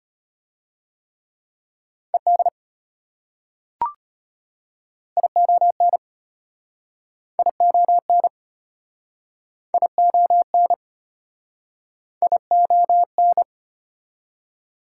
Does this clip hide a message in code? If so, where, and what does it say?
Yes: Morse code from 2.1 to 13.4 s, heard as a keyed tone, about 34 wpm, 700 Hz. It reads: ED ION ION ION ION